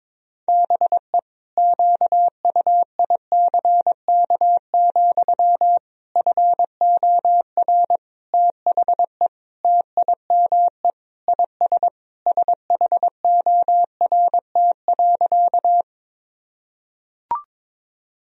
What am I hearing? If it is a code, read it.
Morse code, heard as a keyed tone, about 22 wpm, 700 Hz. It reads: BE QUICK, FOR THE TIME IS SHORT.
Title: Sentences from Top 1000 Words - Review Part 2 - 22wpm